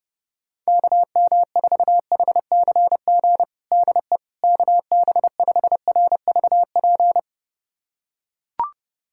Morse code, heard as a keyed tone, about 32 wpm, 700 Hz. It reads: KM4HCG DE KB5RVP
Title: USA Callsign DE Callsign 30wpm